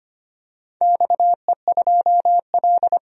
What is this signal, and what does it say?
Morse code, heard as a keyed tone, about 25 wpm, 700 Hz. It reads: XE2L
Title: DXCC-100 Call Signs - 4 Characters - 25wpm